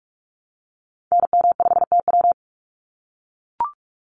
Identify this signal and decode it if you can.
Morse code, heard as a keyed tone, about 44 wpm, 700 Hz. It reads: NM5TW
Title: NA SOTA Activator Callsigns 45wpm